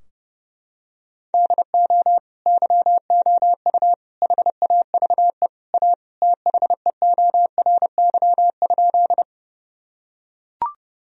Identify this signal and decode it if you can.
Morse code, heard as a keyed tone, about 32 wpm, 700 Hz. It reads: DO YOU HAVE A THEORY?
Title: Sentences from Top 1000 Words - Review Part 3 - 30wpm